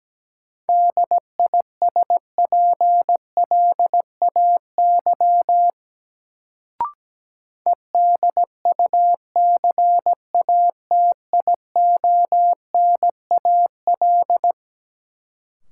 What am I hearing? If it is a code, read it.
Morse code, heard as a keyed tone, about 17 wpm, 700 Hz. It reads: DISPLAY EDUCATIONAL
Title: Sentences from Top 1200 Words - New Word Drill - 17wpm